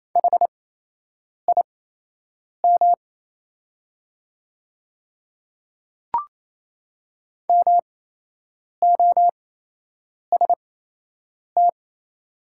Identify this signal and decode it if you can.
Morse code, heard as a keyed tone, about 28 wpm, spaced as for 8 wpm, 700 Hz. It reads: HIM MOST
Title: 8x Character Spacing - Top 100 Words - 28wpm